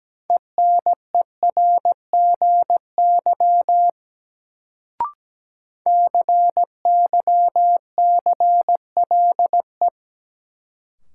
Morse code, heard as a keyed tone, about 17 wpm, 700 Hz. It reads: ENERGY CYCLE
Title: Sentences from Top 900 Words - New Word Drill - 17wpm